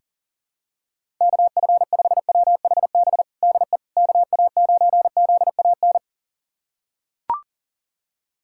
Morse code, heard as a keyed tone, about 40 wpm, 700 Hz. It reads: KF5WHB DE KA9ZAN
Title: USA Callsign DE Callsign - 6 Characters - 40wpm